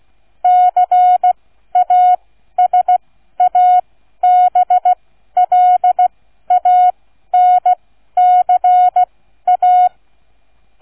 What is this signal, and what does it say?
Morse code, heard as a keyed tone, about 16 wpm, spaced as for 12 wpm, 730 Hz. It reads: CASABLANCA